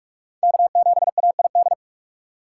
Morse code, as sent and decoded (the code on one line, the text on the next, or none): -.- --... .- .. -..
K7AID